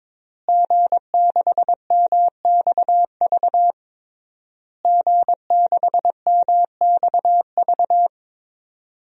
Morse code, heard as a keyed tone, about 22 wpm, 700 Hz. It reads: G6MXV G6MXV
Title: DXCC-100 Call Signs - 5 Characters - Repeated 3x - 22wpm